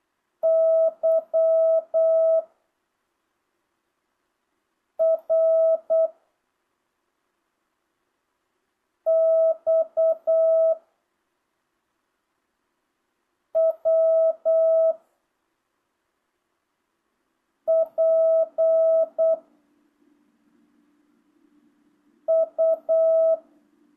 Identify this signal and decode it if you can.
Morse code, heard as a keyed tone, about 8 wpm, 660 Hz. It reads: Y R X W P U
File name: jh8cbh / 2025cw